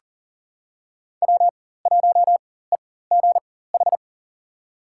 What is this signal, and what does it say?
Morse code, heard as a keyed tone, about 40 wpm, spaced as for 19 wpm, 700 Hz. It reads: W1EGH
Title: USA Call Signs - 5 Characters - 4x Character Spacing - 40wpm